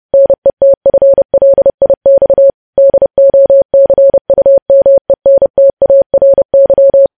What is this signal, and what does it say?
Morse code, heard as a keyed tone, about 30 wpm, 550 Hz. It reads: NETFLIX DOCUMENTARY
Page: news headlines in morse code at 30 wpm